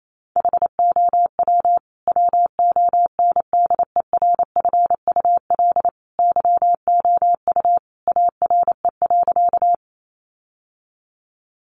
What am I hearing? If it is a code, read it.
Morse code, heard as a keyed tone, about 28 wpm, 700 Hz. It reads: HOW WONDERFUL YOU ARE.
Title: Sentences from Top 1100 Words 28wpm